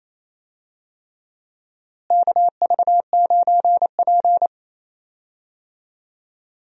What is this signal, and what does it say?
Morse code, heard as a keyed tone, about 28 wpm, 700 Hz. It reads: KV9P
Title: USA Call Signs - 4 Characters - Repeated 3x - 28wpm